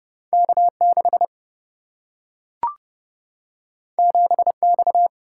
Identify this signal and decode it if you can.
Morse code, heard as a keyed tone, about 30 wpm, 700 Hz. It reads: K6 7X
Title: Single-Letter Single-Number 30wpm